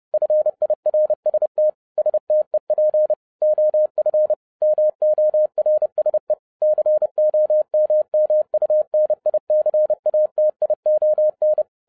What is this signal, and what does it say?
Morse code, heard as a keyed tone, about 30 wpm, 610 Hz. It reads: FIRST STEP OF MORSE COMMUNICATION